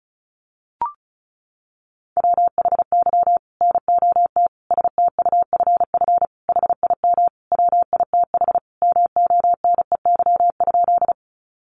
Morse code, heard as a keyed tone, about 34 wpm, 700 Hz. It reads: WHY NOT STUFF HIM WITH MONEY?
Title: Sentences from Top 1400 Words 35wpm